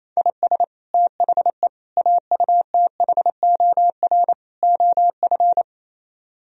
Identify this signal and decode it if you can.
Morse code, heard as a keyed tone, about 32 wpm, 700 Hz. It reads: IS THE AUTHOR OF
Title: Speed-Racing - Sets of 4 Words 28wpm